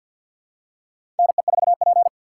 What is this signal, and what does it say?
Morse code, heard as a keyed tone, about 47 wpm, 700 Hz. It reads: NE4P
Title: USA Call Signs - 4 Characters - 3x Character Spacing - 50wpm